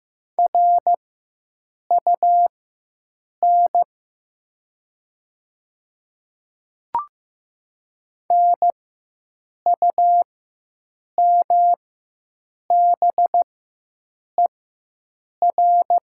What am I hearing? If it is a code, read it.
Morse code, heard as a keyed tone, about 15 wpm, spaced as for 7 wpm, 700 Hz. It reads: RUN NUMBER
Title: 4x Character Spacing - Top 300 Words - 15wpm